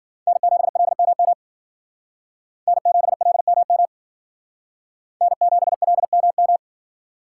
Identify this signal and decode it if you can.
Morse code, heard as a keyed tone, about 51 wpm, 700 Hz. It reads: N7LMM N7LMM N7LMM